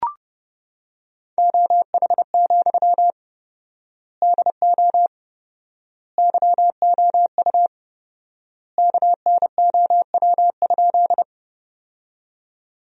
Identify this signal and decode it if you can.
Morse code, heard as a keyed tone, about 30 wpm, 700 Hz. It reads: OH, DO YOU KNOW?
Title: 4x Word-Spacing - Sentences from Top 500 Words - 30wpm